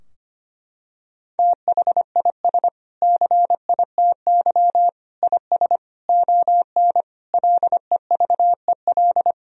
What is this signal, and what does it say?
Morse code, heard as a keyed tone, about 25 wpm, 700 Hz. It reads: THIS CITY IS ON LEVEL
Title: Sentences from Top 1000 Words - Review Part 4 - 25wpm